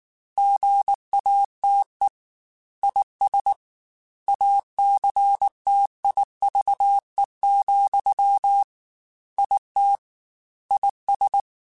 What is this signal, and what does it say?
Morse code, heard as a keyed tone, about 19 wpm, 800 Hz. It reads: GATE IS ACTIVE, IT IS
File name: Technical Subjects / Technical Subjects-mp3s / AND Gate @18